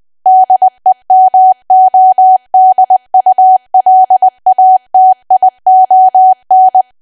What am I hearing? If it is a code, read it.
Morse code, heard as a keyed tone, about 20 wpm, 750 Hz. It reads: DEMODULATION
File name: cwa / W1AW-Level3 / 20WPM